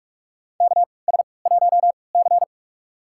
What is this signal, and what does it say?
Morse code, heard as a keyed tone, about 47 wpm, spaced as for 26 wpm, 700 Hz. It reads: KS1C
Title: USA Call Signs - 4 Characters - 3x Character Spacing - 45wpm